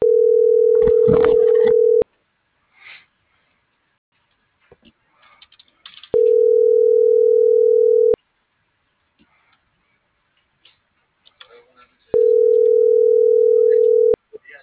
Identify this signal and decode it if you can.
Ringback tone (North America), 440 + 480 Hz.